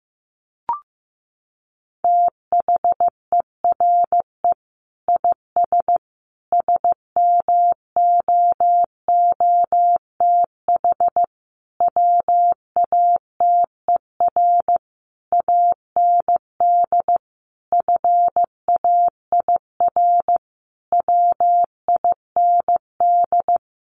Morse code, heard as a keyed tone, about 15 wpm, 700 Hz. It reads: THERE IS SMOOTH WATER AND FAIR WIND
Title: Sentences from Top 1400 Words 15wpm